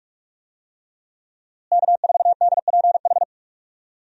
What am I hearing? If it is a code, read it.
Morse code, heard as a keyed tone, about 45 wpm, 700 Hz. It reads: K4DPH
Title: USA Call Signs - 5 Characters - Repeated 3x - 45wpm